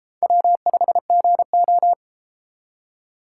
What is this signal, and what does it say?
Morse code, heard as a keyed tone, about 32 wpm, 700 Hz. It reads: W5GO